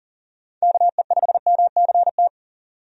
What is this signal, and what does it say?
Morse code, heard as a keyed tone, about 40 wpm, 700 Hz. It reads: KE5MCT
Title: USA Call Signs - 6 Characters - 40wpm